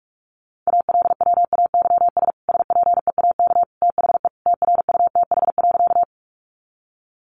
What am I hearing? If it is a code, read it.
Morse code, heard as a keyed tone, about 44 wpm, 700 Hz. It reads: ALWAYS SPEAK THE TRUTH.
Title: Sentences from Top 1100 Words 45wpm